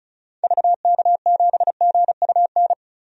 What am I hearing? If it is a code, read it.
Morse code, heard as a keyed tone, about 36 wpm, 700 Hz. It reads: VK7GUN
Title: DXCC-100 Call Signs - 6 Characters - 35wpm